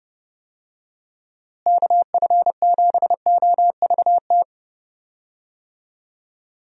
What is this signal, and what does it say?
Morse code, heard as a keyed tone, about 30 wpm, 700 Hz. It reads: KF7OVT
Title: USA Call Signs - 6 Characters - Repeated 2x - 30wpm